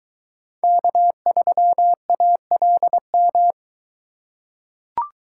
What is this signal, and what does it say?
Morse code, heard as a keyed tone, about 21 wpm, 700 Hz. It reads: K3ALM